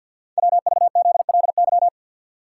Morse code, heard as a keyed tone, about 49 wpm, 700 Hz. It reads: WVZLY
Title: Code Groups - 5 Letters - 50wpm